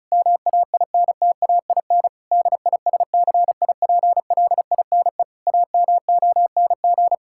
Morse code, heard as a keyed tone, about 34 wpm, 700 Hz. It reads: MAINTAIN DISCIPLINE AMONG
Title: Sentences from Top 1000 Words - Review Part 6 - 35wpm